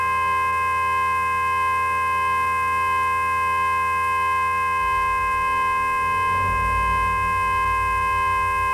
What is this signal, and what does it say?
Emergency Alert System (SAME) tones (the 1050 Hz attention tone), with noise. It is heard from 0.9 to 7.6 s.